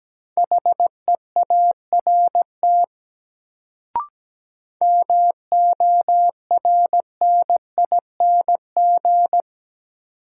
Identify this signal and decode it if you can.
Morse code, heard as a keyed tone, about 17 wpm, 700 Hz. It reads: HEART MORNING
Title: Sets of 3 Words - New Word Drill - 17wpm